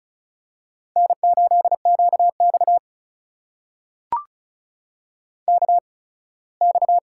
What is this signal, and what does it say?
Morse code, heard as a keyed tone, about 36 wpm, spaced as for 21 wpm, 700 Hz. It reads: N8QX KX
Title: USA Call Signs - 4 Characters - 8x Character Spacing - 35wpm